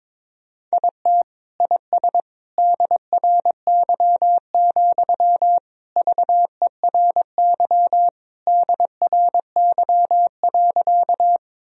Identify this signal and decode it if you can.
Morse code, heard as a keyed tone, about 21 wpm, 700 Hz. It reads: IT IS DRY, VERY DRY.
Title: Sentences from Top 1000 Words - Review Part 3 - 22wpm